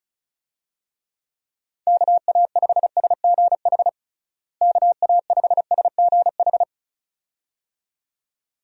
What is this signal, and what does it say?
Morse code, heard as a keyed tone, about 34 wpm, 700 Hz. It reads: KA5SGH KA5SGH
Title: USA Call Signs - 6 Characters - Repeated 2x - 35wpm